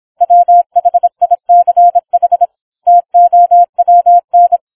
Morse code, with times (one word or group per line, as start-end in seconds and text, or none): 0.2-2.5 WHICH
2.9-4.6 TOWN